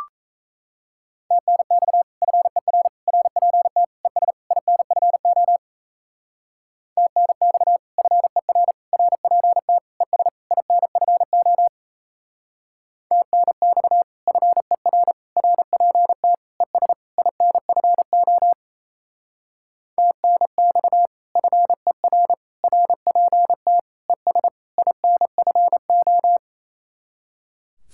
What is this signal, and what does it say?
Morse code, heard as a keyed tone, about 36 wpm, 700 Hz. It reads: TNX FER RPT ES INFO TNX FER RPT ES INFO TNX FER RPT ES INFO TNX FER RPT ES INFO